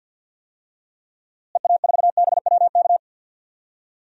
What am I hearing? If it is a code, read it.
Morse code, heard as a keyed tone, about 51 wpm, 700 Hz. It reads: EA4BWK
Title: DXCC-100 Call Signs - 6 Characters - Repeated 2x - 50wpm